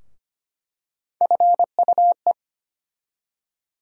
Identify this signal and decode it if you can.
Morse code, heard as a keyed tone, about 25 wpm, 700 Hz. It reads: FUE
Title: USA Call Sign Postfixes - 3 Letters 25wpm